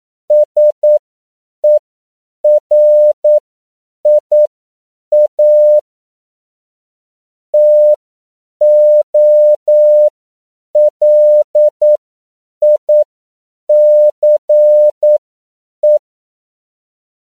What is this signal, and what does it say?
Morse code, heard as a keyed tone, about 9 wpm, spaced as for 7 wpm, 600 Hz. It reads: SERIA TOLICE
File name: CW / 10PPM